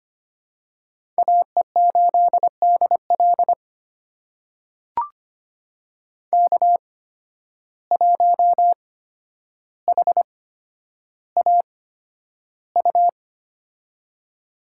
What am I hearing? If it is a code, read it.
Morse code, heard as a keyed tone, about 25 wpm, spaced as for 11 wpm, 700 Hz. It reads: AE8DL K1HAU
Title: USA Call Signs - 8x Character Spacing - 25wpm